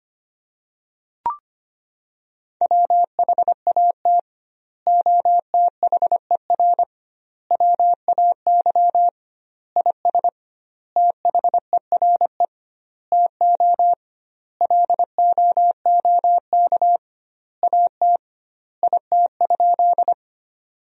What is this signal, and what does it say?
Morse code, heard as a keyed tone, about 25 wpm, 700 Hz. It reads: WHAT OTHER WAY IS THERE TO LOOK AT IT?
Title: 2x Word-Spacing - Sentences from Top 100 Words - 25wpm